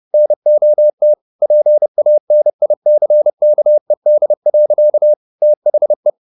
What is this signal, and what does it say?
Morse code, heard as a keyed tone, about 30 wpm, 600 Hz. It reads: NOT PANICKED. THE
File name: the old man and the sea / the old man and the sea 1 30wpm